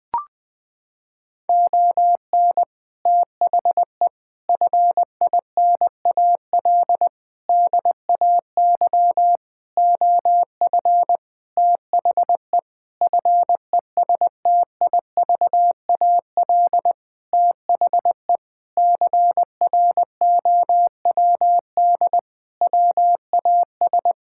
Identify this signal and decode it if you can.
Morse code, heard as a keyed tone, about 20 wpm, 700 Hz. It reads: ON THE FINAL DAY OF THE FESTIVAL THE CROWD WAS
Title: Sentences with 6-Word Phrases - 20wpm